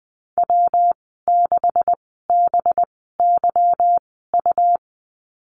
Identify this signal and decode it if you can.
Morse code, heard as a keyed tone, about 20 wpm, spaced as for 14 wpm, 700 Hz. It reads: W6BYU
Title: 193 Lesson 31g - Letter v - Call Signs 20wpm